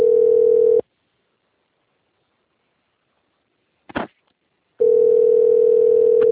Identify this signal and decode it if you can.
Ringback tone (North America), 440 + 480 Hz.